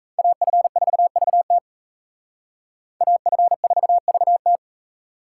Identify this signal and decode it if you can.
Morse code, heard as a keyed tone, about 32 wpm, 700 Hz. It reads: AF4VT AF4VT
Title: Speed Racing - USA 2x2 Call Signs 28wpm